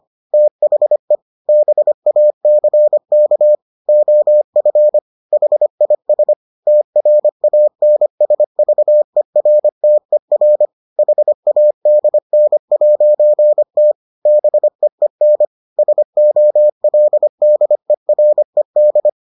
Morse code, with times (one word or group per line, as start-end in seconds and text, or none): 0.3-1.2 THE
1.5-3.6 BACK
3.9-5.0 OF
5.3-6.3 HIS
6.7-10.7 TRANSVERTER
11.0-13.9 HADN'T
14.3-15.4 BEEN
15.8-19.1 SOLDERED